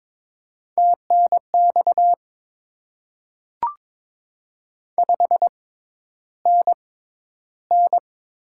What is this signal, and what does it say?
Morse code, heard as a keyed tone, about 22 wpm, spaced as for 11 wpm, 700 Hz. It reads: TNX 5NN
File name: Top 20 QSO Elements - 6x Character Spacing - 22wpm